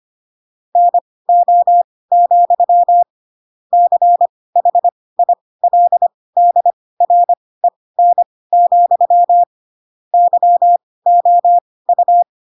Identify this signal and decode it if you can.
Morse code, heard as a keyed tone, about 25 wpm, spaced as for 18 wpm, 700 Hz. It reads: NO, CHILDREN, YOU
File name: Pinocchio / Pinocchio ch1 12 25 ik1hge